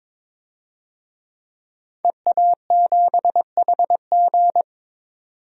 Morse code, heard as a keyed tone, about 22 wpm, 700 Hz. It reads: EA7HG
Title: DXCC-100 Call Signs - 5 Characters - 22wpm